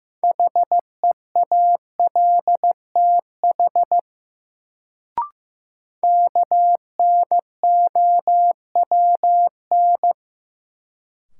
Morse code, heard as a keyed tone, about 15 wpm, 700 Hz. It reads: HEALTH KNOWN
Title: Sentences from Top 600 Words - New Word Drill - 15wpm